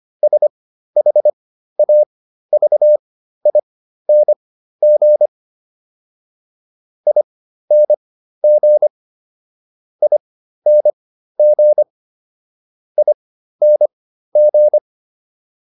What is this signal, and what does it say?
Morse code, heard as a keyed tone, about 25 wpm, spaced as for 13 wpm, 600 Hz. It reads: SHAVING ING ING ING